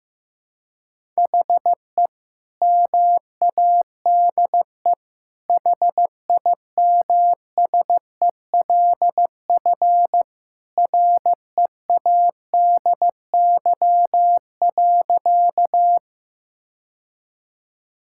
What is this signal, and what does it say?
Morse code, heard as a keyed tone, about 15 wpm, 700 Hz. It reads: HE MADE HIMSELF READY.
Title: Sentences from Top 1600 Words 15wpm